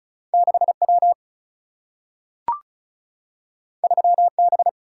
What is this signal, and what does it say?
Morse code, heard as a keyed tone, about 36 wpm, 700 Hz. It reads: 6W 3B
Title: Single-Letter Single-Number 35wpm